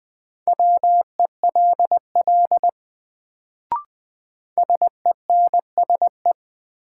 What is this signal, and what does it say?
Morse code, heard as a keyed tone, about 20 wpm, 700 Hz. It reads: WELL SENSE
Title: Sets of 2 Words - New Word Drill - 20wpm